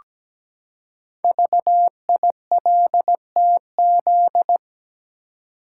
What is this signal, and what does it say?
Morse code, heard as a keyed tone, about 17 wpm, 700 Hz. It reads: VILTZ
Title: Code Groups - 5 Letters - 17wpm